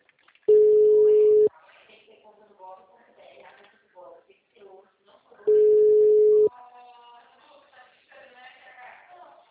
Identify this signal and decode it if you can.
Ringback tone (Europe, 425 Hz), with speech.